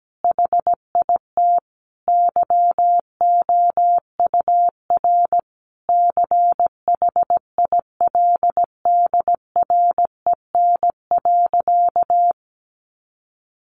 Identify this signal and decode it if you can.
Morse code, heard as a keyed tone, about 17 wpm, 700 Hz. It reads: HIT YOUR CHILDREN.
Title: Sentences from Top 1200 Words 17wpm